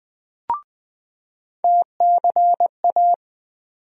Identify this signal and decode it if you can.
Morse code, heard as a keyed tone, about 20 wpm, 700 Hz. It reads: TCA
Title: USA Call Sign Postfixes - 3 Letters 20wpm